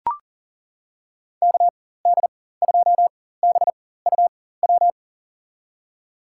Morse code, heard as a keyed tone, about 39 wpm, spaced as for 18 wpm, 700 Hz. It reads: KD2BUW
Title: USA Call Signs - 6 Characters - 4x Character Spacing - 40wpm